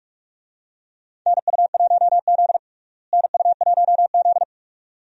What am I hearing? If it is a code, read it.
Morse code, heard as a keyed tone, about 47 wpm, 700 Hz. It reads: NU1Z NU1Z
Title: USA Call Signs - 4 Characters - Repeated 2x - 45wpm